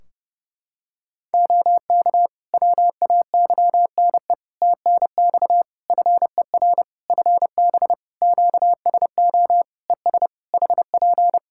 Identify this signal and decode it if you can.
Morse code, heard as a keyed tone, about 30 wpm, 700 Hz. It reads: OK WAYNE TNX FER FB QSO ES HP
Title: Practice QSO - Exchange 4 - Ending - 30wpm